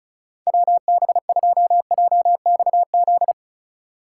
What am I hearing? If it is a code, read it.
Morse code, heard as a keyed tone, about 34 wpm, 700 Hz. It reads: WB2JXZ
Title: USA Call Signs - 8x Character Spacing - 35wpm